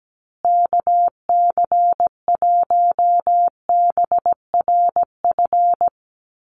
Morse code, heard as a keyed tone, about 17 wpm, 700 Hz. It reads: KC1BRF
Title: USA Call Signs Repeated 2x 17wpm